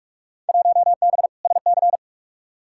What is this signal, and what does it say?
Morse code, heard as a keyed tone, about 46 wpm, 700 Hz. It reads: 1B SC
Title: Field Day Exchanges - 45wpm